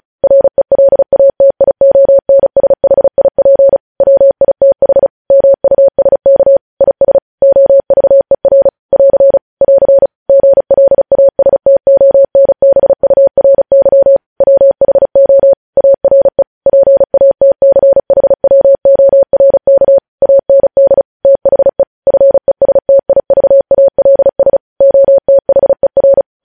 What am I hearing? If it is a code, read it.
Morse code, heard as a keyed tone, about 36 wpm, 550 Hz. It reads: RELATIONSHIP WITH MUSK IS OVER <AR> <AR> GLASTONBURY WHO ARE PATCHWORK AND THE FESTIVALS OTHER
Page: news headlines in morse code at 35 wpm